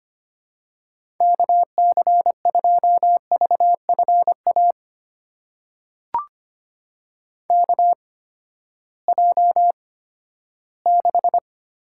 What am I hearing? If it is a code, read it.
Morse code, heard as a keyed tone, about 25 wpm, spaced as for 14 wpm, 700 Hz. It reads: KC2VFA KJ6
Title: USA Call Signs - 6 Characters - 8x Character Spacing - 25wpm